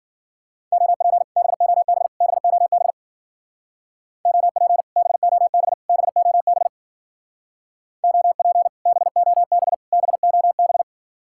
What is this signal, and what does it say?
Morse code, heard as a keyed tone, about 50 wpm, 700 Hz. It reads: OP BOB BOB OP BOB BOB OP BOB BOB
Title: Speed-Racing Common QSO Phrases - 40wpm